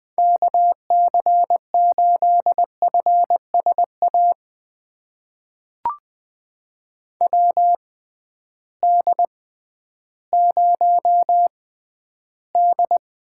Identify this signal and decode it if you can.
Morse code, heard as a keyed tone, about 20 wpm, spaced as for 12 wpm, 700 Hz. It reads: KC8FSA WD0D
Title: USA Call Signs - 6 Characters - 6x Character Spacing - 20wpm